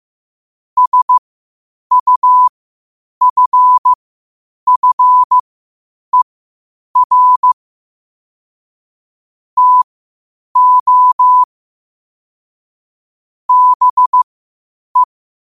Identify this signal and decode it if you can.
Morse code, heard as a keyed tone, about 15 wpm, spaced as for 8 wpm, 1000 Hz. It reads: SUFFER TO BE
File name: qotdfiles / 8wpm